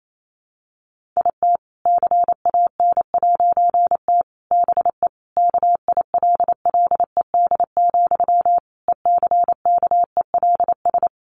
Morse code, heard as a keyed tone, about 28 wpm, 700 Hz. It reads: IT CAN'T BE KILLED, ECKELS
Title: A Sound Of Thunder 28wpm